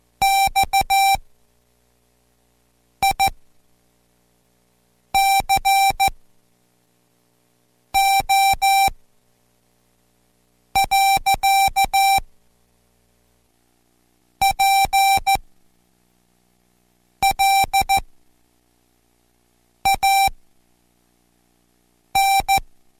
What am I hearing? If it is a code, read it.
Morse code, heard as a keyed tone, about 14 wpm, spaced as for 4 wpm, 780 Hz. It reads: XICO. PLAN